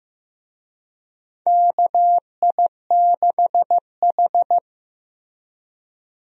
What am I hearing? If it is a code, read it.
Morse code, heard as a keyed tone, about 15 wpm, 700 Hz. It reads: KI6H
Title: USA Call Signs - 4 Characters - Repeated 3x - 15wpm